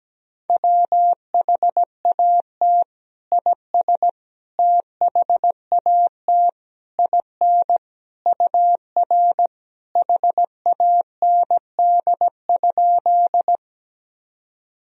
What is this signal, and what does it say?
Morse code, heard as a keyed tone, about 17 wpm, 700 Hz. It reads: WHAT IS THAT IN UR HAND?
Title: Sentences from Top 200 Words and Abbreviations 17wpm